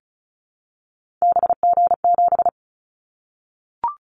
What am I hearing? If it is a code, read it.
Morse code, heard as a keyed tone, about 36 wpm, 700 Hz. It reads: BG7
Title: Callsign Prefixes for Top 100 DXCC - 35wpm